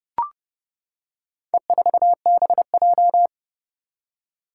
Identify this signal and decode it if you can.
Morse code, heard as a keyed tone, about 30 wpm, 700 Hz. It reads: E4BJ
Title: Four Challenging Letter-Number Combinations 30wpm